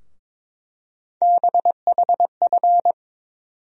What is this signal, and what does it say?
Morse code, heard as a keyed tone, about 22 wpm, 700 Hz. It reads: BHF
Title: USA Call Sign Postfixes - 3 Letters 22wpm